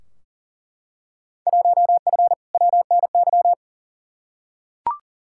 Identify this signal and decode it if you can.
Morse code, heard as a keyed tone, about 40 wpm, 700 Hz. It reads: 1F WNY
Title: Field Day Exchanges - 40wpm